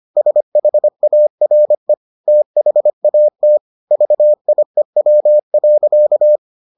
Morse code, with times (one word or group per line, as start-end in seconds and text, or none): 0.2-1.9 SHARE
2.3-3.6 THAT
3.9-6.4 VIEW.